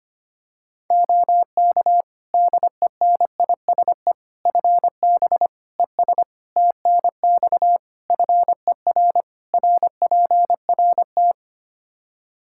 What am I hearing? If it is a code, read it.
Morse code, heard as a keyed tone, about 25 wpm, 700 Hz. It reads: OK DENISE FB ES TNX FER RPRT
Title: Practice QSO - Exchange 2 - Info - 25wpm